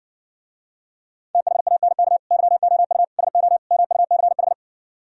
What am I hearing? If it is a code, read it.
Morse code, heard as a keyed tone, about 69 wpm, 700 Hz. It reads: THANK YOU SO MUCH